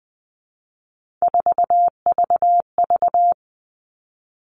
Morse code, heard as a keyed tone, about 20 wpm, 700 Hz. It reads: 4VV
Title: All Cut Numbers - 3-Digit 20wpm